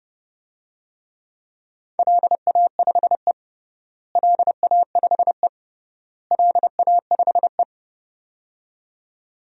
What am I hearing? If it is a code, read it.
Morse code, heard as a keyed tone, about 30 wpm, 700 Hz. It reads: LA5E LA5E LA5E
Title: DXCC-100 Call Signs - 4 Characters - Repeated 3x - 30wpm